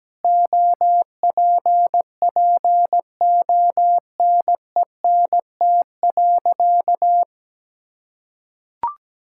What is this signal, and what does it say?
Morse code, heard as a keyed tone, about 17 wpm, 700 Hz. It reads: OPPONENT.
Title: Pangram Challenge v2 - 17wpm